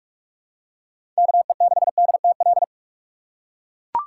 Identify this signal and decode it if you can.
Morse code, heard as a keyed tone, about 43 wpm, 700 Hz. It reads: KE6DTL
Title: USA Call Signs - 4x Character Spacing - 45wpm